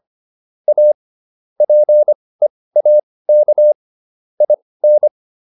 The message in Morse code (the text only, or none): A PEAK IN